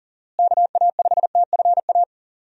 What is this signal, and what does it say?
Morse code, heard as a keyed tone, about 40 wpm, 700 Hz. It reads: KA5TFA
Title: USA Call Signs - 6 Characters - Repeated 2x - 40wpm